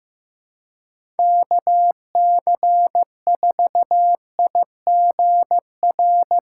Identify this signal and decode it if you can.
Morse code, heard as a keyed tone, about 15 wpm, 700 Hz. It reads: KC4IGR
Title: USA Call Signs - 4x Character Spacing - 15wpm